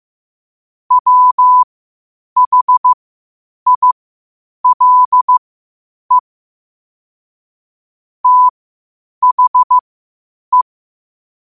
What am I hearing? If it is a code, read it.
Morse code, heard as a keyed tone, about 15 wpm, spaced as for 8 wpm, 1000 Hz. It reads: WHILE THE